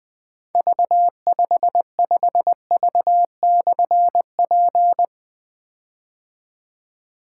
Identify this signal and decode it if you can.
Morse code, heard as a keyed tone, about 20 wpm, 700 Hz. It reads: V55V/P
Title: Difficult Call Signs 20wpm